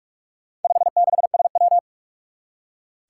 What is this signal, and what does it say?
Morse code, heard as a keyed tone, about 44 wpm, 700 Hz. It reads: 56SW
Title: Four Challenging Letter-Number Combinations 45wpm